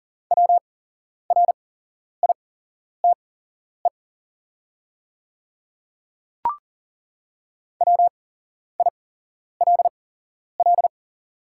Morse code, heard as a keyed tone, about 39 wpm, spaced as for 11 wpm, 700 Hz. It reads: WRITE WILL
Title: 8x Character Spacing - Top 100 Words - 40wpm